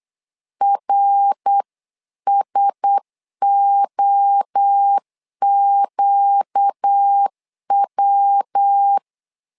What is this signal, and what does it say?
Morse code, heard as a keyed tone, about 8 wpm, 800 Hz. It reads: RSOQW
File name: LEZIONI / 4 Q1S3 8wpm 3037